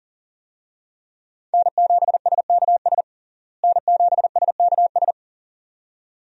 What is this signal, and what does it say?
Morse code, heard as a keyed tone, about 40 wpm, 700 Hz. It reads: N7SKS N7SKS
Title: USA Call Signs - 5 Characters - Repeated 2x - 40wpm